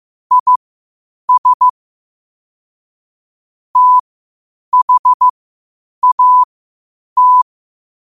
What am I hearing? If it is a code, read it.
Morse code, heard as a keyed tone, about 15 wpm, spaced as for 8 wpm, 1000 Hz. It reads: IS THAT
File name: qotdfiles / 8wpm